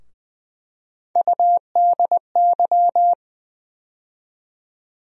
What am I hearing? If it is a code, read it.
Morse code, heard as a keyed tone, about 20 wpm, 700 Hz. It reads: UDY
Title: USA Call Sign Postfixes - 3 Letters 20wpm